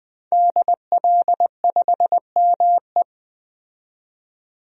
Morse code, heard as a keyed tone, about 20 wpm, 700 Hz. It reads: DL5ME
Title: DXCC-100 Call Signs - 5 Characters - 20wpm